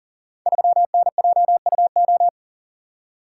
Morse code, heard as a keyed tone, about 40 wpm, 700 Hz. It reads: 3NJUO
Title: Code Groups - 5 Letter:Numbers - 40wpm